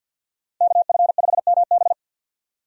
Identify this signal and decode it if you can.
Morse code, heard as a keyed tone, about 51 wpm, 700 Hz. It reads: KF5MB
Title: USA Call Signs - 5 Characters - Repeated 2x - 50wpm